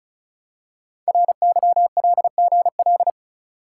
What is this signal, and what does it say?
Morse code, heard as a keyed tone, about 36 wpm, 700 Hz. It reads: RYLGL
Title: Code Groups - 5 Letters - 35wpm